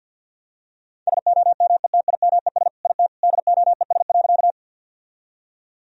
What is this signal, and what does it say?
Morse code, heard as a keyed tone, about 50 wpm, 700 Hz. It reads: SOMETIMES IT DOES.
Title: Sentences from Top 1000 Words - Review Part 3 - 50wpm